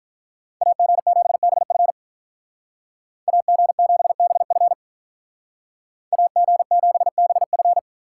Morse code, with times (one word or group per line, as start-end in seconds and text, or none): 0.6-1.9 AG7BF
3.3-4.7 AG7BF
6.1-7.8 AG7BF